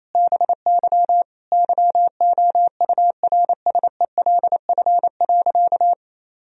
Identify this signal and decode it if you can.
Morse code, heard as a keyed tone, about 27 wpm, 700 Hz. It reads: BY YOURSELF.
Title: Sentences from Top 1000 Words - Review Part 4 - 28wpm